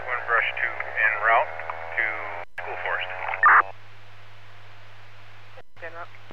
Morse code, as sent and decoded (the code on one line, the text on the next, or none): none